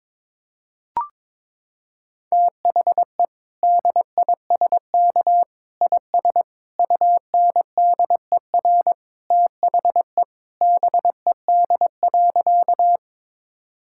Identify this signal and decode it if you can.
Morse code, heard as a keyed tone, about 22 wpm, 700 Hz. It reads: THE DISK IS UNDER THE BED.
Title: Sentences from Top 1000 Words - Review Part 6 - 22wpm